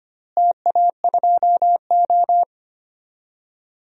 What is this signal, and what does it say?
Morse code, heard as a keyed tone, about 25 wpm, 700 Hz. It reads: TA2O
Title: DXCC-100 Call Signs - 4 Characters - Repeated 3x - 25wpm